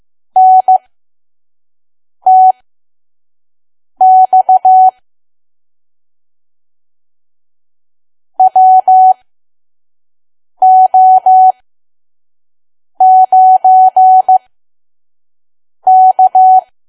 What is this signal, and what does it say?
Morse code, heard as a keyed tone, about 15 wpm, spaced as for 5 wpm, 750 Hz. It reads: NTX WO9K
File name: Archive / 5 WPM files / PM